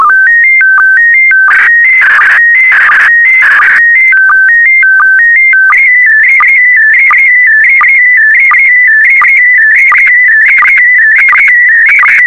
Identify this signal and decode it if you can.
SSTV (slow-scan television), PD 90.
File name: sstv / PD90-VK4MSL